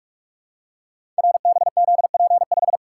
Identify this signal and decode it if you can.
Morse code, heard as a keyed tone, about 44 wpm, 700 Hz. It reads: RBZP5
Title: Code Groups - 5 Letter:Numbers - 45wpm